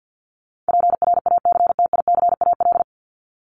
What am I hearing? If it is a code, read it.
Morse code, heard as a keyed tone, about 48 wpm, 700 Hz. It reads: PRACTICAL
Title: LY Suffix - Top 100 Words with Base Words 50wpm